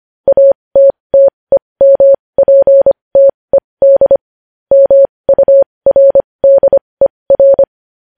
Morse code, heard as a keyed tone, about 25 wpm, spaced as for 20 wpm, 550 Hz. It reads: ATTEMPTED MURDER
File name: podcast / News headlines-1x-25x20